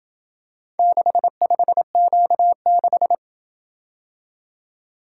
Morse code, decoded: 65Q6

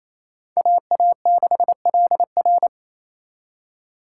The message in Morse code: AA6LR

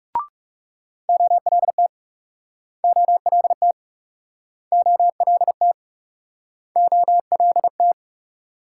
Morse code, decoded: OLT OLT OLT OLT

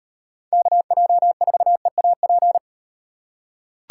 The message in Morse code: KJ4EAP

